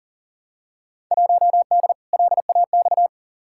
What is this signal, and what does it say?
Morse code, heard as a keyed tone, about 40 wpm, 700 Hz. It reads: 1D LAX
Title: Field Day Exchanges - 40wpm